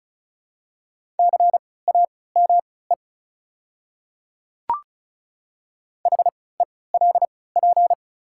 Morse code, heard as a keyed tone, about 36 wpm, spaced as for 20 wpm, 690 Hz. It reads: CAME HELP
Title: 3x Character Spacing - Top 300 Words - 35wpm